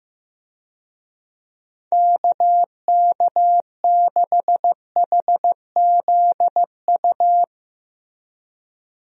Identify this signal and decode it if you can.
Morse code, heard as a keyed tone, about 15 wpm, 700 Hz. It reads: KK6HZU
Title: USA Call Signs - 6 Characters - Repeated 3x - 15wpm